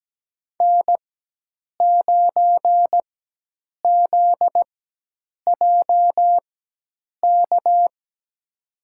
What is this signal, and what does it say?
Morse code, heard as a keyed tone, about 17 wpm, spaced as for 8 wpm, 700 Hz. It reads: N9ZJK
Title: USA Call Signs - 4x Character Spacing - 17wpm